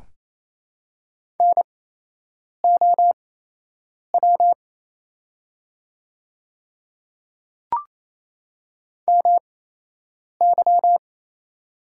Morse code, heard as a keyed tone, about 28 wpm, spaced as for 8 wpm, 700 Hz. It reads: NOW MY
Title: 8x Character Spacing - Top 100 Words - 28wpm